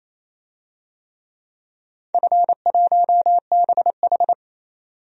Morse code, heard as a keyed tone, about 28 wpm, 700 Hz. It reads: F1BH
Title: DXCC-100 Call Signs - 4 Characters - 28wpm